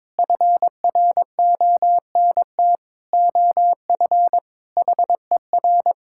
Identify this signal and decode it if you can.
Morse code, heard as a keyed tone, about 22 wpm, 700 Hz. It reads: FRONT OF HER
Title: Sentences from Top 1000 Words - Review Part 5 - 22wpm